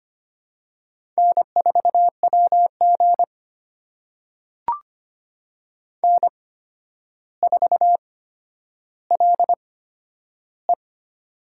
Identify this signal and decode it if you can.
Morse code, heard as a keyed tone, about 25 wpm, spaced as for 11 wpm, 700 Hz. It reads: N4WG N4LE
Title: USA Call Signs - 4 Characters - 8x Character Spacing - 25wpm